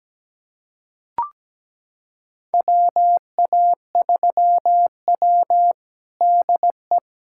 Morse code, heard as a keyed tone, about 17 wpm, 700 Hz. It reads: WA3W DE
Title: USA Callsign DE Callsign - 4 Characters - 17wpm